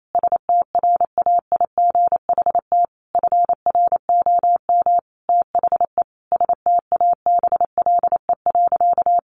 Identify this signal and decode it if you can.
Morse code, heard as a keyed tone, about 28 wpm, 700 Hz. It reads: STRAIGHT FROM THE STABLE.
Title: Sentences from Top 1500 Words 28wpm